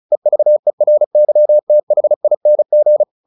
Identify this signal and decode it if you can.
Morse code, heard as a keyed tone, about 34 wpm, 600 Hz. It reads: EVERYTHING
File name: the old man and the sea 2 35wpm